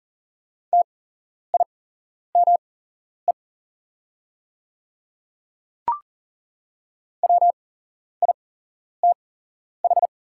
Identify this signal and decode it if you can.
Morse code, heard as a keyed tone, about 40 wpm, spaced as for 11 wpm, 700 Hz. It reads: TIME WITH
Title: 8x Character Spacing - Top 100 Words - 40wpm